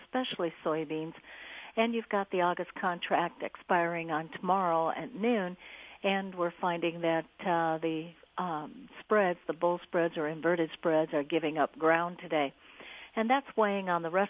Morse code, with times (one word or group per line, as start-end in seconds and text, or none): none